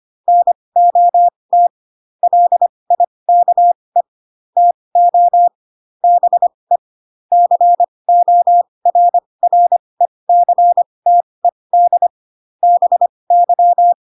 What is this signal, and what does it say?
Morse code, heard as a keyed tone, about 25 wpm, spaced as for 20 wpm, 700 Hz. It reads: NOT LIKE TO BE CORRECTED BY
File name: Pinocchio / Pinocchio ch4 15 25 ik1hge